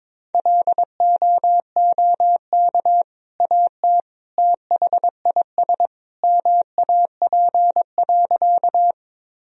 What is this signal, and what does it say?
Morse code, heard as a keyed tone, about 22 wpm, 700 Hz. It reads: LOOK AT THIS MAP.
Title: Sentences from Top 1000 Words - Review Part 2 - 22wpm